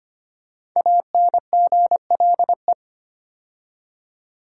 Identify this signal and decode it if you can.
Morse code, heard as a keyed tone, about 25 wpm, 700 Hz. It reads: ANGLE
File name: 5 Letter Words - First 3 Letters Given - 25wpm